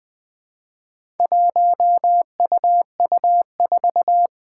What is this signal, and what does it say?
Morse code, heard as a keyed tone, about 20 wpm, 700 Hz. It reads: 1UU4
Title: Four Challenging Letter-Number Combinations 20wpm